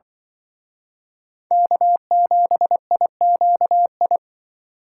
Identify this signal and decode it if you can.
Morse code, heard as a keyed tone, about 24 wpm, 700 Hz. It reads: K7IQI
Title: Speed Racing - USA Call Signs 20wpm